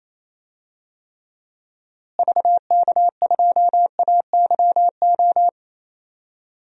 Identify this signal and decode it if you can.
Morse code, heard as a keyed tone, about 28 wpm, 700 Hz. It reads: VK2AYO
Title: DXCC-100 Call Signs - 6 Characters - Repeated 3x - 28wpm